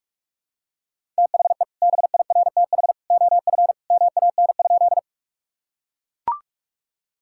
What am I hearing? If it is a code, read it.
Morse code, heard as a keyed tone, about 41 wpm, 700 Hz. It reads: THE BIRTH OF MAN?
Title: Sentences from Top 1000 Words - Review Part 5 - 45wpm